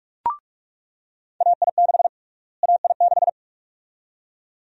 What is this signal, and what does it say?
Morse code, heard as a keyed tone, about 44 wpm, 700 Hz. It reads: AI6 AI6
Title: USA Call Sign Prefixes - Repeated 2x - 45wpm